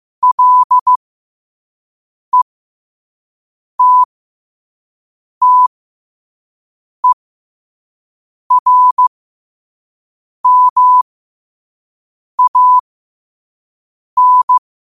Morse code, heard as a keyed tone, about 15 wpm, spaced as for 5 wpm, 1000 Hz. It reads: LETTERMAN
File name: qotdfiles / 5wpm